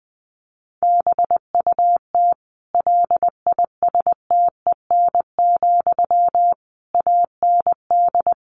Morse code, heard as a keyed tone, about 20 wpm, 700 Hz. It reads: BUT LISTEN, AND
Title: Sentences from Top 600 Words 20wpm